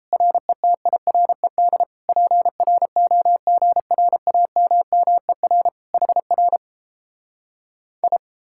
Morse code, heard as a keyed tone, about 38 wpm, 700 Hz. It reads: RETIRED PROGRAMMER HR I